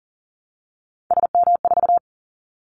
Morse code, heard as a keyed tone, about 40 wpm, 700 Hz. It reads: SM4
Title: Callsign Prefixes for Top 25 DXCC - Repeated 3x - 40wpm